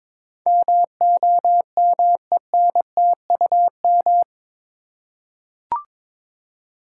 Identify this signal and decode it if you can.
Morse code, heard as a keyed tone, about 22 wpm, 700 Hz. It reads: MOMENTUM
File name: practice / 22wpm / Eight Letter Words - 22wpm